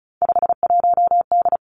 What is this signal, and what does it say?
Morse code, heard as a keyed tone, about 36 wpm, 700 Hz. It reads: 51D
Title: Three Challenging Letter-Number Combinations 35wpm